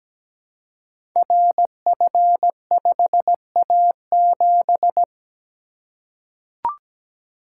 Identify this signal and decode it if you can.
Morse code, heard as a keyed tone, about 17 wpm, 700 Hz. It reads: RF5A7
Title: Code Groups - 5 Letter:Numbers - 17wpm